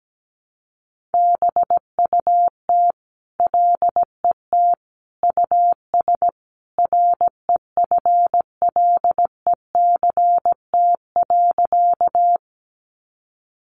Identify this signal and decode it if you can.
Morse code, heard as a keyed tone, about 17 wpm, 700 Hz. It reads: BUT LET US REFLECT.
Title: Sentences from Top 1500 Words 17wpm